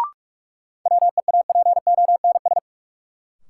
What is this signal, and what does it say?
Morse code, heard as a keyed tone, about 41 wpm, 700 Hz. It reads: WEAPONS
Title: Sets of 5 Words Encore - New Word Drill - 45wpm